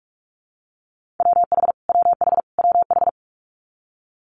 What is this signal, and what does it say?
Morse code, heard as a keyed tone, about 46 wpm, 700 Hz. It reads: WH WH WH
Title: Multi-letter Phonemes Repeated 3x 45wpm